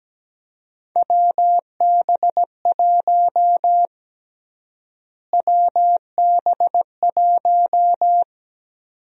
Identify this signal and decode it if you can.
Morse code, heard as a keyed tone, about 17 wpm, 700 Hz. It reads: WB1 WB1
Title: USA Call Sign Prefixes - Repeated 2x - 17wpm